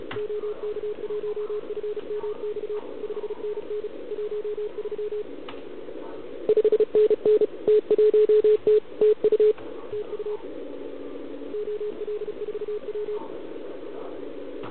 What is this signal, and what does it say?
Morse code, heard as a keyed tone, about 34 wpm, 420 Hz, with radio noise. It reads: OK1FMX 5NN 13 5NN T1T TU TU OL4W
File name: truSDX/KX3a